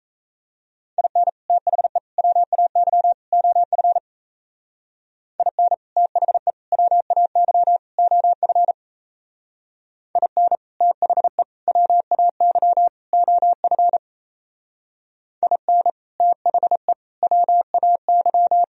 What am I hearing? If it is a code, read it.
Morse code, heard as a keyed tone, about 37 wpm, 700 Hz. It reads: IN THE WAY OF IN THE WAY OF IN THE WAY OF IN THE WAY